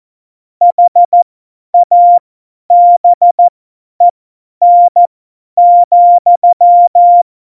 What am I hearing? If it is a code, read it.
Morse code, heard as a keyed tone, about 14 wpm, spaced as for 10 wpm, 700 Hz. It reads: HABEN,
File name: Morsekurs / Witze / Witz 07. November 2017, 50 F70